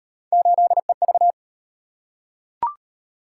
Morse code, heard as a keyed tone, about 31 wpm, 700 Hz. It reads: 8EV